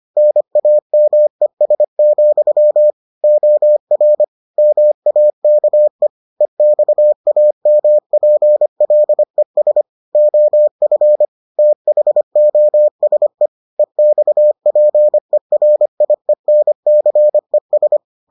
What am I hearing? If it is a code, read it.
Morse code, heard as a keyed tone, about 25 wpm, 600 Hz. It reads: NAMES, OR MAKE EXAMPLES OF THOSE EXPERIENCES